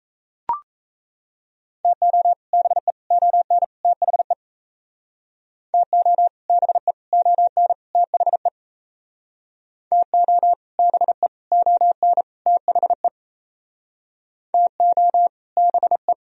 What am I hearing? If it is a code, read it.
Morse code, heard as a keyed tone, about 37 wpm, 700 Hz. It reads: TO BE ON THE TO BE ON THE TO BE ON THE TO BE